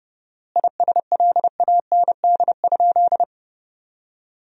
Morse code, heard as a keyed tone, about 32 wpm, 700 Hz. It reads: ISLAND?
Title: 4x Word-Spacing - Sentences from Top 500 Words - 30wpm